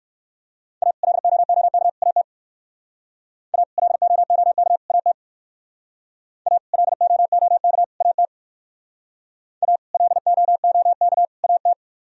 Morse code, heard as a keyed tone, about 58 wpm, 700 Hz. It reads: A LOOK AT A LOOK AT A LOOK AT A LOOK AT